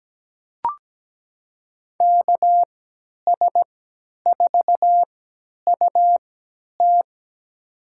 Morse code, heard as a keyed tone, about 17 wpm, spaced as for 10 wpm, 700 Hz. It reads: KS4UT
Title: USA Call Signs - 5 Characters - 3x Character Spacing - 17wpm